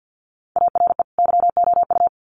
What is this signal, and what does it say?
Morse code, heard as a keyed tone, about 50 wpm, 700 Hz. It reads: ARE YOU
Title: Sets of 2 Words 50wpm